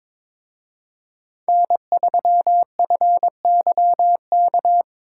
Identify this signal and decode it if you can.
Morse code, heard as a keyed tone, about 25 wpm, 700 Hz. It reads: N3FYK
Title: Speed Racing - USA 1x3 Call Signs 22wpm